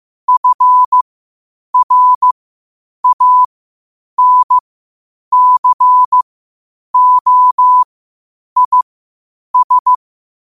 Morse code, heard as a keyed tone, about 15 wpm, spaced as for 8 wpm, 1000 Hz. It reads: FRANCOIS
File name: qotdfiles / 8wpm